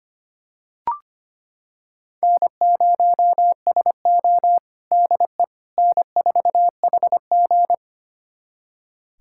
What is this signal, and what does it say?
Morse code, heard as a keyed tone, about 25 wpm, 700 Hz. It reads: N0SO DE N4HG
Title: USA Callsign DE Callsign - 4 Characters - 25wpm